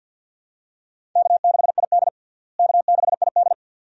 Morse code, heard as a keyed tone, about 51 wpm, 700 Hz. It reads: K6ID K6ID
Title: USA Call Signs - 4 Characters - Repeated 2x - 50wpm